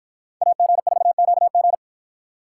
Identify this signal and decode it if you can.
Morse code, heard as a keyed tone, about 53 wpm, 700 Hz. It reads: AG4QG